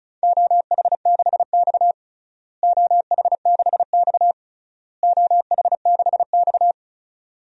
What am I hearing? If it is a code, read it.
Morse code, heard as a keyed tone, about 36 wpm, 700 Hz. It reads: OH6X OH6X OH6X
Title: DXCC-100 Call Signs - 4 Characters - Repeated 3x - 35wpm